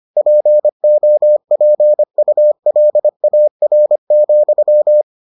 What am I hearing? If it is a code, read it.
Morse code, heard as a keyed tone, about 25 wpm, 600 Hz. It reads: POPULAR,